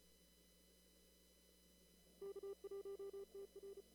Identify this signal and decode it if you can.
Morse code, heard as a keyed tone, about 34 wpm, 400 Hz. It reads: K1TR
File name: CW / kt5j wpx cw-op-k5pi